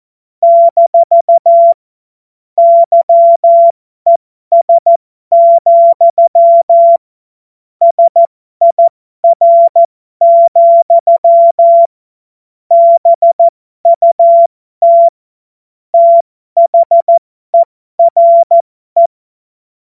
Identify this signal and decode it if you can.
Morse code, heard as a keyed tone, about 14 wpm, spaced as for 12 wpm, 680 Hz. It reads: - YES, SIR, BUT THERE